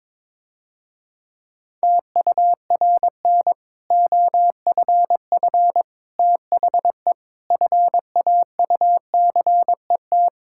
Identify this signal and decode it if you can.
Morse code, heard as a keyed tone, about 21 wpm, 700 Hz. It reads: TURN OFF THE FAUCET
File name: Difficult Sentences with Hints - 22wpm